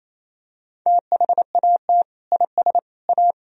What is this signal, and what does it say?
Morse code, heard as a keyed tone, about 28 wpm, 700 Hz. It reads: THAT IS A
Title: Sentences from Top 1000 Words - Review Part 1 - 28wpm